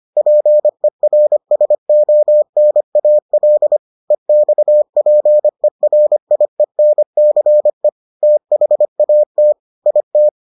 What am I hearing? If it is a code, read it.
Morse code, heard as a keyed tone, about 25 wpm, 600 Hz. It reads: PERSONAL EXPERIENCE THAT IT